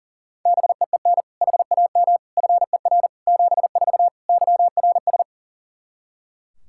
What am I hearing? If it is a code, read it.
Morse code, heard as a keyed tone, about 40 wpm, 700 Hz. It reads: BEEN HAM FER 74 YRS
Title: Practice QSO - Exchange 3 - More Info - 40wpm